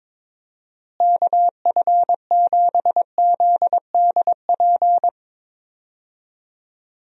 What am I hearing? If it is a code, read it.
Morse code, heard as a keyed tone, about 21 wpm, 700 Hz. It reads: KF7ZDP
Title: USA Call Signs - 6 Characters - Repeated 3x - 22wpm